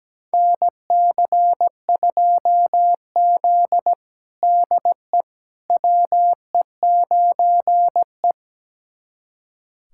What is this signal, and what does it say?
Morse code, heard as a keyed tone, about 17 wpm, 700 Hz. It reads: NC2Z DE WE9E
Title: USA Callsign DE Callsign - 4 Characters - 17wpm